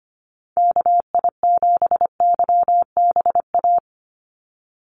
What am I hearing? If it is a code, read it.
Morse code, heard as a keyed tone, about 25 wpm, 700 Hz. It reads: KI7YBA
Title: USA Call Signs 25wpm